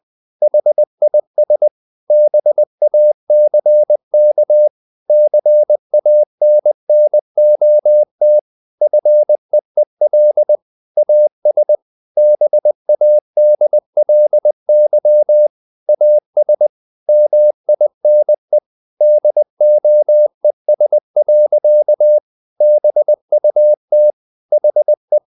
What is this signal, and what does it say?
Morse code, heard as a keyed tone, about 20 wpm, 600 Hz. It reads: HIS BACK CANNOT FEEL AS BADLY AS MINE DOES. BUT HE
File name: the old man and the sea / the old man and the sea part 1 20wpm